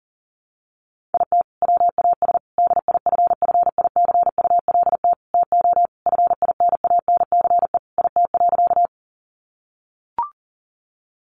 Morse code, heard as a keyed tone, about 37 wpm, 700 Hz. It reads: IT WAS DIFFICULT TO FINANCE IT.
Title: Sentences from Top 1500 Words 40wpm